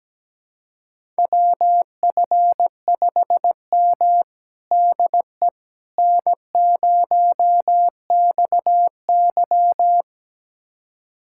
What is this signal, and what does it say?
Morse code, heard as a keyed tone, about 17 wpm, 700 Hz. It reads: WF5M DE N0XY
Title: USA Callsign DE Callsign - 4 Characters - 17wpm